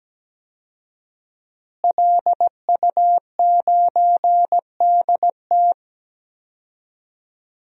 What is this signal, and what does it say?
Morse code, heard as a keyed tone, about 17 wpm, 700 Hz. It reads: LU9DT
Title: DXCC-100 Call Signs - 5 Characters - Repeated 2x - 17wpm